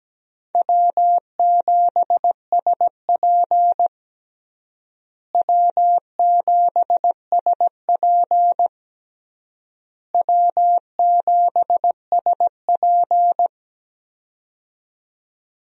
Morse code, heard as a keyed tone, about 17 wpm, 700 Hz. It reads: W7SP W7SP W7SP
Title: USA Call Signs - 4 Characters - Repeated 3x - 17wpm